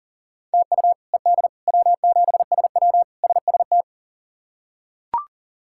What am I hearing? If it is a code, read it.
Morse code, heard as a keyed tone, about 40 wpm, 700 Hz. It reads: TU ED W7SW SST
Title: K1USN SST QSO Practice - 40wpm